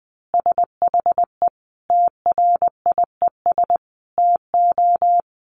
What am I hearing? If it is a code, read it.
Morse code, heard as a keyed tone, about 20 wpm, 700 Hz. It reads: SHE TRIES TO
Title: Easy Phrases 20wpm